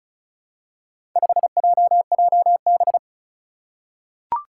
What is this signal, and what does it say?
Morse code, heard as a keyed tone, about 36 wpm, 700 Hz. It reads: 5JJB
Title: Four Challenging Letter-Number Combinations 35wpm